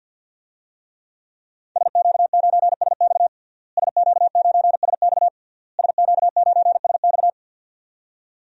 Morse code, heard as a keyed tone, about 47 wpm, 700 Hz. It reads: SQ9SX SQ9SX SQ9SX
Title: DXCC-100 Call Signs - 5 Characters - Repeated 3x - 50wpm